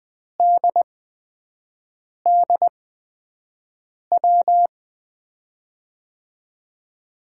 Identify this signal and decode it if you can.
Morse code, heard as a keyed tone, about 20 wpm, spaced as for 5 wpm, 700 Hz. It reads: DDW
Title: USA Call Signs - 6 Characters - 8x Character Spacing - 20wpm